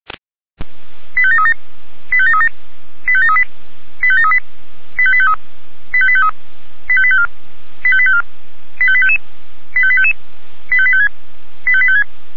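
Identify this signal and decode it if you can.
Five-tone selective call (ZVEI), heard as quick runs of five tones, tones of about 80 ms. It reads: 85737 85737 85738 85738 85853 85853 85854 85854 85855 85855 85856 85856
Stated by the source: Five-tone ZVEI selective call received on an SDR.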